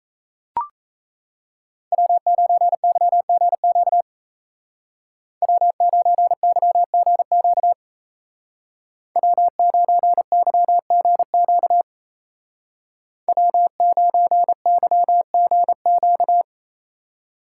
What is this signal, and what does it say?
Morse code, heard as a keyed tone, about 33 wpm, 700 Hz. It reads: W9YGQ W9YGQ W9YGQ W9YGQ
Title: Speed Racing - USA 1x3 Call Signs 28wpm